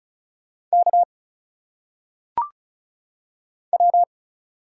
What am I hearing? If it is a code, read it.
Morse code, heard as a keyed tone, about 36 wpm, 700 Hz. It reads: K W
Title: Single Letters - Phonetic Pronunciation 35wpm